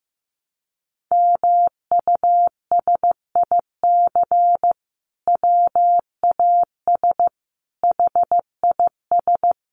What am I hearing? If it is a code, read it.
Morse code, heard as a keyed tone, about 15 wpm, 700 Hz. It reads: MUSIC WAS HIS
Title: Sentences from Top 1300 Words 15wpm